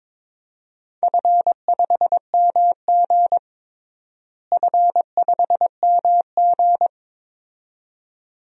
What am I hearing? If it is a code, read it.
Morse code, heard as a keyed tone, about 22 wpm, 700 Hz. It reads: F5MG F5MG
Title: DXCC-100 Call Signs - 4 Characters - Repeated 2x - 22wpm